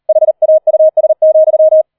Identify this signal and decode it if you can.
Morse code, heard as a keyed tone, about 38 wpm, 630 Hz. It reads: HAUS,
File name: 004 Vierter Streich 40wpm